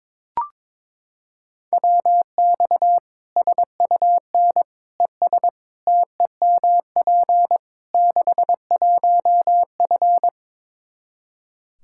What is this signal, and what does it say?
Morse code, heard as a keyed tone, about 22 wpm, 700 Hz. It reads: WX SUN ES TEMP 61F
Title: Practice QSO - Exchange 2 - Info - 22wpm